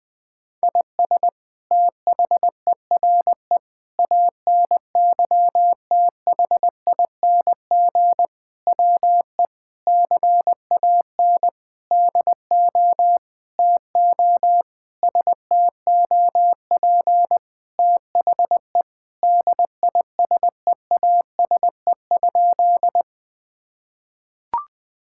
Morse code, heard as a keyed tone, about 20 wpm, 700 Hz. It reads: IS THERE ANYTHING WE CAN DO TO STOP THE DISEASE?
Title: Sentences from Top 1000 Words - Review Part 6 - 20wpm